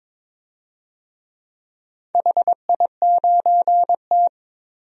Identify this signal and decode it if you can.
Morse code, heard as a keyed tone, about 23 wpm, 690 Hz. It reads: HI9T
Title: DXCC-100 Call Signs - 4 Characters - 22wpm